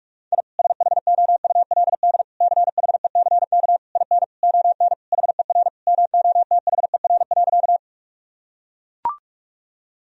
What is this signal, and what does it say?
Morse code, heard as a keyed tone, about 41 wpm, 700 Hz. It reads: I SHOULD CHECK IN ON HER MOTHER.
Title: Sentences from Top 1000 Words - Review Part 3 - 45wpm